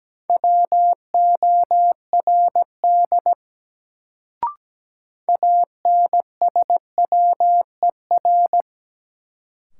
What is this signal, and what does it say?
Morse code, heard as a keyed tone, about 17 wpm, 700 Hz. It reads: WORD ANSWER
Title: Sentences from Top 300 Words - New Word Drill - 17wpm